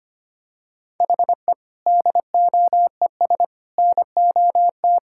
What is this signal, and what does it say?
Morse code, heard as a keyed tone, about 25 wpm, 700 Hz. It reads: HE DOES NOT
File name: practice / 25wpm / Sentences with 3-Word Phrases - 25wpm